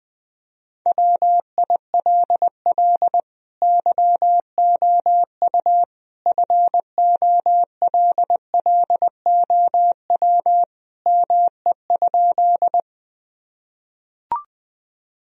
Morse code, heard as a keyed tone, about 20 wpm, 700 Hz. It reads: WILL YOU FOLLOW ME?
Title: Sentences from Top 1000 Words - Review Part 1 - 20wpm